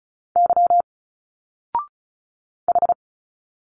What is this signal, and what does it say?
Morse code, heard as a keyed tone, about 34 wpm, 700 Hz. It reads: Y H
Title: Single Letters 35wpm